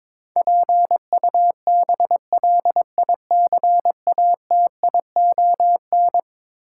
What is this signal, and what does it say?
Morse code, heard as a keyed tone, about 22 wpm, 700 Hz. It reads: PUBLICATION
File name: practice / 22wpm / Eleven Letter Words - 22wpm